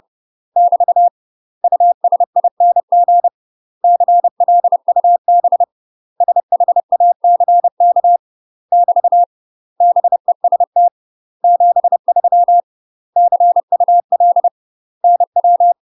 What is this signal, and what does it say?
Morse code, heard as a keyed tone, about 30 wpm, 700 Hz. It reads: <BT> USING CLUB SHACK <BT> BEST 73 CUL NW
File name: UK MASTER QSOs TRAINING / 30WPM / UK MASTER QSOs 0008